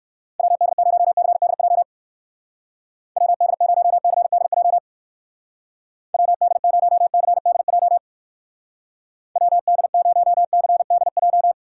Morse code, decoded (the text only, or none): WD0CDJ WD0CDJ WD0CDJ WD0CDJ